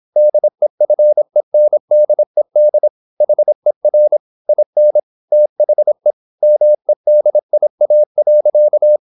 Morse code, heard as a keyed tone, about 26 wpm, 600 Hz. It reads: DEFENDED HER IN THE MEDIA.
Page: Never 26wpm | CW med Gnister